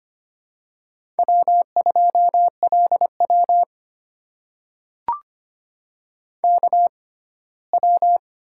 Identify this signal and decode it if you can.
Morse code, heard as a keyed tone, about 25 wpm, spaced as for 17 wpm, 700 Hz. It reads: W2LW KW
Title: USA Call Signs - 4 Characters - 6x Character Spacing - 25wpm